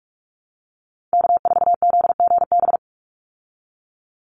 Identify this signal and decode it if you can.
Morse code, heard as a keyed tone, about 44 wpm, 700 Hz. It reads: K4ZGB
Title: Call Signs Repeated 2x - 45wpm